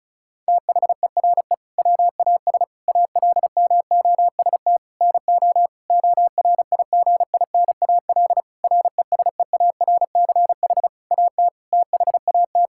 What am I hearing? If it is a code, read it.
Morse code, heard as a keyed tone, about 34 wpm, 700 Hz. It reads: THERE WAS ALMOST NO ORIGINAL RESEARCH AT THAT
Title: Sentences from Top 1000 Words - Review Part 6 - 35wpm